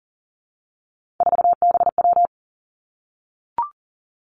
Morse code, heard as a keyed tone, about 38 wpm, 700 Hz. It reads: 4BW
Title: Three Challenging Letter-Number Combinations 40wpm